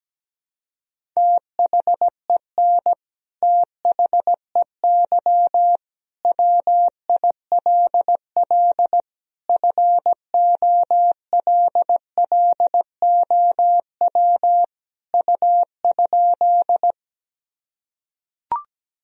Morse code, heard as a keyed tone, about 17 wpm, 700 Hz. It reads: THEN THEY WILL FOLLOW U?
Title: Sentences from Top 200 Words and Abbreviations 17wpm